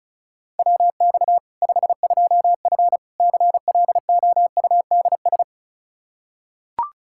Morse code, heard as a keyed tone, about 36 wpm, 700 Hz. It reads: WX 52F CLOUDS
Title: K1USN SST QSO Practice 35wpm